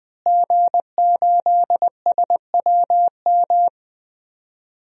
Morse code, heard as a keyed tone, about 20 wpm, 700 Hz. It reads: G8SWM
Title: DXCC-100 Call Signs - 5 Characters - Repeated 3x - 20wpm